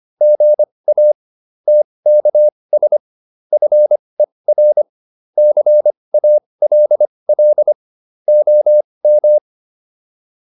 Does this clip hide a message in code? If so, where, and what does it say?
Yes: Morse code from 0.2 to 9.4 s, heard as a keyed tone, about 25 wpm, spaced as for 20 wpm, 600 Hz. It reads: GA TKS FER CALL OM